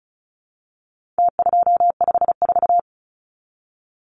Morse code, heard as a keyed tone, about 34 wpm, 700 Hz. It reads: T254
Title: Numbers - Contest Style - 4-Digit 35wpm